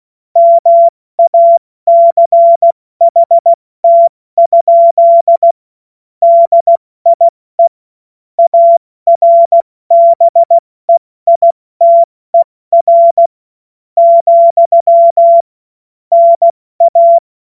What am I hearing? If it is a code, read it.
Morse code, heard as a keyed tone, about 16 wpm, spaced as for 14 wpm, 680 Hz. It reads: MACHT? DIE ARBEITER, NA
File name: Morsekurs / Witze / Witz 22. August 2023 70 F80